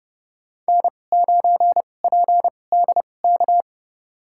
Morse code, heard as a keyed tone, about 30 wpm, spaced as for 22 wpm, 690 Hz. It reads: N9PDK
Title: USA Call Signs - 5 Characters - 2x Character Spacing - 30wpm